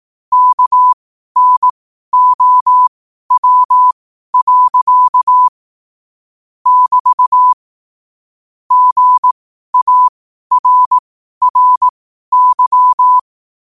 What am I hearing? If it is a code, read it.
Morse code, heard as a keyed tone, about 18 wpm, spaced as for 12 wpm, 1000 Hz. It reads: KNOW. <BT> GARRY